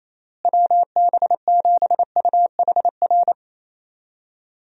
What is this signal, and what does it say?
Morse code, heard as a keyed tone, about 28 wpm, 700 Hz. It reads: WB7UHR
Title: USA Call Signs - 6 Characters - Repeated 3x - 28wpm